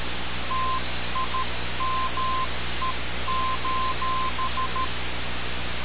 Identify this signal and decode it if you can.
Morse code, heard as a keyed tone, about 13 wpm, 1040 Hz, with noise. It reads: TIME:
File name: radiocap / REC z